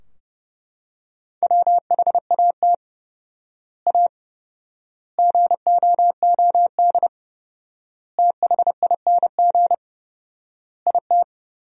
Morse code, heard as a keyed tone, about 30 wpm, 700 Hz. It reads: WHAT A GOOD THING IT
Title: 4x Word-Spacing - Sentences from Top 200 Words - 30wpm